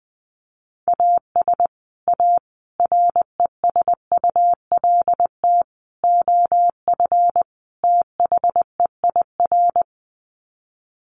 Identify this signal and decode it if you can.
Morse code, heard as a keyed tone, about 20 wpm, 700 Hz. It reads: AS A RESULT OF THEIR
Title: Sets of 5 Words Encore2 20wpm